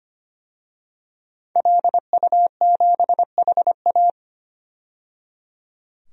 Morse code, heard as a keyed tone, about 25 wpm, 700 Hz. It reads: LU7HA
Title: DXCC-100 Call Signs - 5 Characters - 25wpm